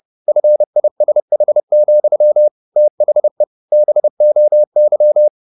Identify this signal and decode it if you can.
Morse code, heard as a keyed tone, about 30 wpm, 600 Hz. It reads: FISH, THE BOY